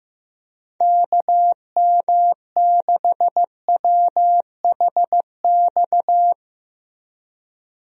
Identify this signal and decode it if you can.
Morse code, heard as a keyed tone, about 16 wpm, 700 Hz. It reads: KM6WHX
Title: USA Call Signs - 6 Characters - Repeated 3x - 15wpm